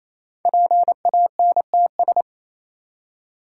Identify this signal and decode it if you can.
Morse code, heard as a keyed tone, about 27 wpm, 700 Hz. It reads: PANTS
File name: Easy Words - Speed-Racing with Hints - 22wpm